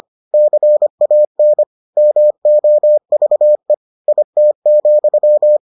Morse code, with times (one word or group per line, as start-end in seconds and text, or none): 0.3-1.6 CAN
2.0-3.7 MOVE
4.1-5.6 IT,